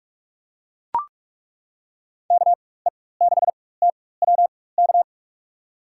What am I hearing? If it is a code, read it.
Morse code, heard as a keyed tone, about 42 wpm, spaced as for 21 wpm, 700 Hz. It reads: KE6TWK
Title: USA Call Signs - 6 Characters - 4x Character Spacing - 45wpm